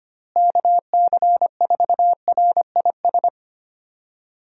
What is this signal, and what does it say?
Morse code, heard as a keyed tone, about 25 wpm, 700 Hz. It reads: KC4RIS
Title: USA Call Signs - 6 Characters - Repeated 3x - 25wpm